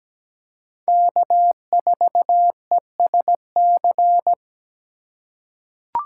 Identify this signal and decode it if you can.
Morse code, heard as a keyed tone, about 17 wpm, 700 Hz. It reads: K4ESC
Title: USA Call Signs - 5 Characters - 3x Character Spacing - 17wpm